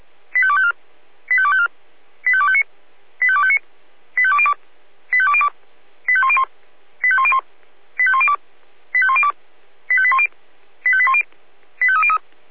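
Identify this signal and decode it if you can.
Five-tone selective call (ZVEI), heard as quick runs of five tones, tones of about 70 ms. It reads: 85254 85254 85259 85259 85202 85202 85101 85101 85102 85102 85810 85810 85303
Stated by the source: ZVEI five-tone selective call received on SDR.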